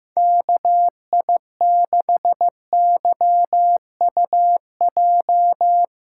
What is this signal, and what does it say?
Morse code, heard as a keyed tone, about 15 wpm, 700 Hz. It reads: KI6YUJ
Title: USA Callsign DE Callsign 15wpm